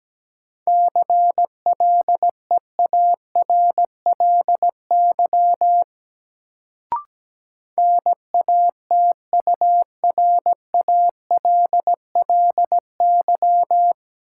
Morse code, heard as a keyed tone, about 17 wpm, 700 Hz. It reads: CLEARLY NATURALLY
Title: Sentences from Top 1300 Words - New Word Drill Alt - 17wpm